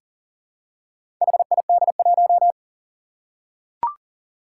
Morse code, heard as a keyed tone, about 43 wpm, 700 Hz. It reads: HID1
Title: Four Challenging Letter-Number Combinations 40wpm